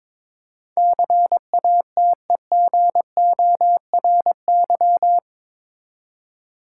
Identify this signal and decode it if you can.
Morse code, heard as a keyed tone, about 21 wpm, 700 Hz. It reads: CATEGORY
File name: practice / 22wpm / Eight Letter Words - 22wpm